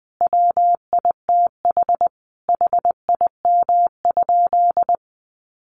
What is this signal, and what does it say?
Morse code, heard as a keyed tone, about 20 wpm, 700 Hz. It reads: WITH HIM?
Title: Sentences from Top 1400 Words 20wpm